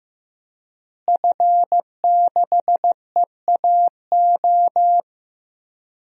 Morse code, heard as a keyed tone, about 15 wpm, 700 Hz. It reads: F6EAO
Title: DXCC-100 Call Signs - 5 Characters - Repeated 2x - 15wpm